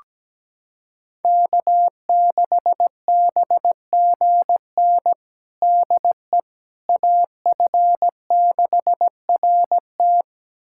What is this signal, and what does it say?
Morse code, heard as a keyed tone, about 17 wpm, 700 Hz. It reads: K6BGN DE AF6RT
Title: USA Callsign DE Callsign - 5 Characters - 17wpm